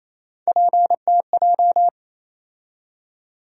Morse code, heard as a keyed tone, about 27 wpm, 700 Hz. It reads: PTJ
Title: USA Call Sign Postfixes - 3 Letters 28wpm